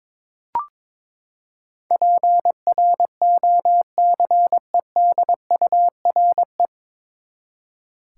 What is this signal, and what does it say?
Morse code, heard as a keyed tone, about 22 wpm, 700 Hz. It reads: PROCEDURE
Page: Morse Code Ninja practice 22wpm